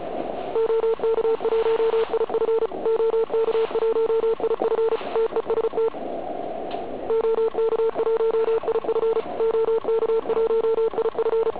Morse code, heard as a keyed tone, about 33 wpm, 440 Hz, with noise. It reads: OK1IF OK1IF TEST OK1IF OK1IF
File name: Archiv / CUC106 / cuc106 PILEUP